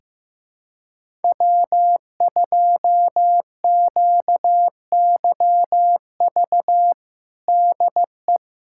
Morse code, heard as a keyed tone, about 15 wpm, 700 Hz. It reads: W2QYV DE
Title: USA Callsign DE Callsign - 5 Characters - 15wpm